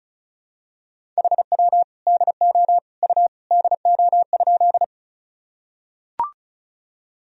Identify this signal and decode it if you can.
Morse code, heard as a keyed tone, about 36 wpm, 700 Hz. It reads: HW DO U DO?
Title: Sentences from Top 100 Words and Abbreviations 35wpm